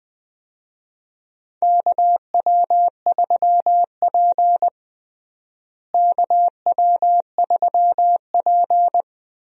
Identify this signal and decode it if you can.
Morse code, heard as a keyed tone, about 21 wpm, 700 Hz. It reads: KW3P KW3P
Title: USA Call Signs - 4 Characters - Repeated 2x - 20wpm